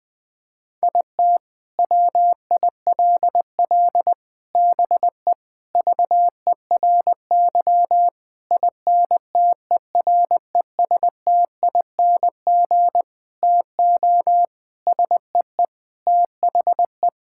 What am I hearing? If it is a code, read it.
Morse code, heard as a keyed tone, about 19 wpm, 700 Hz. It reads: IT WILL BE VERY INTERESTING TO SEE THE
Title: Sentences with 7-Word Phrases - Spoken Beginning - 20wpm